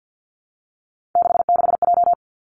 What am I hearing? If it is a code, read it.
Morse code, heard as a keyed tone, about 50 wpm, 700 Hz. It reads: 66J
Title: Three Challenging Letter-Number Combinations 50wpm